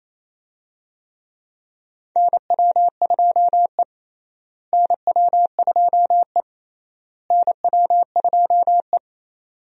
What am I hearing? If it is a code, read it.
Morse code, heard as a keyed tone, about 27 wpm, 700 Hz. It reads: NW2E NW2E NW2E
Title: USA Call Signs - 4 Characters - Repeated 3x - 28wpm